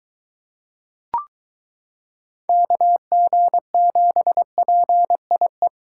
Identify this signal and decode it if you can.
Morse code, heard as a keyed tone, about 23 wpm, 700 Hz. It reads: KG7PIE